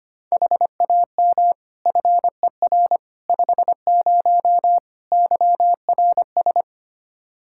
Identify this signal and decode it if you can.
Morse code, heard as a keyed tone, about 25 wpm, 700 Hz. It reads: HAM FER 50 YRS
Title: Practice QSO - Exchange 3 - More Info - 25wpm